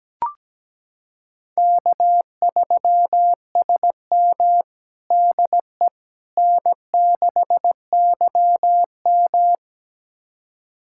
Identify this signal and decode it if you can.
Morse code, heard as a keyed tone, about 17 wpm, 700 Hz. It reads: K3SM DE N6YM
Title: USA Callsign DE Callsign - 4 Characters - 17wpm